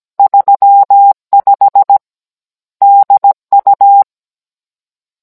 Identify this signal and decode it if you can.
Morse code, heard as a keyed tone, about 17 wpm, 800 Hz. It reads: 35 DU